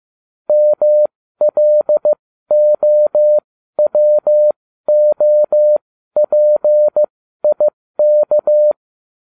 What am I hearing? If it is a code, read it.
Morse code, heard as a keyed tone, about 15 wpm, spaced as for 12 wpm, 600 Hz. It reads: MLOWOPIK